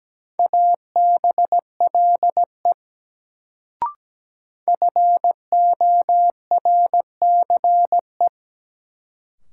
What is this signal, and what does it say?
Morse code, heard as a keyed tone, about 17 wpm, 700 Hz. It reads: ABLE FORCE
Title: Sentences from Top 700 Words - New Word Drill - 17wpm